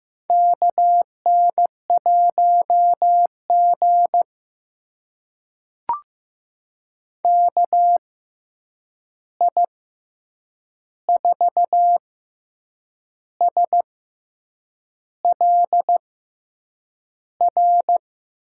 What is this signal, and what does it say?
Morse code, heard as a keyed tone, about 15 wpm, spaced as for 7 wpm, 700 Hz. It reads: KN1G KI4SLR
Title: USA Call Signs - 6x Character Spacing - 15wpm